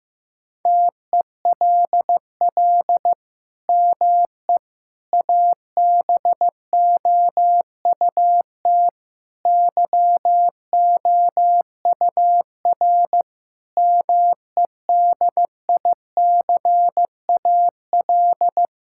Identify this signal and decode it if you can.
Morse code, heard as a keyed tone, about 15 wpm, 700 Hz. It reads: TELL ME ABOUT YOUR MEDICAL
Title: Sentences from Top 1000 Words - Review Part 5 - 15wpm